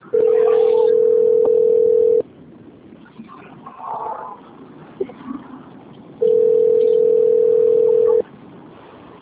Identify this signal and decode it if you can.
Ringback tone (North America), 440 + 480 Hz.
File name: out- - - - - .20267